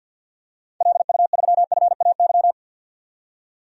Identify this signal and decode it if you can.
Morse code, heard as a keyed tone, about 48 wpm, 700 Hz. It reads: LU3FAY
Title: DXCC-100 Call Signs - 6 Characters - 50wpm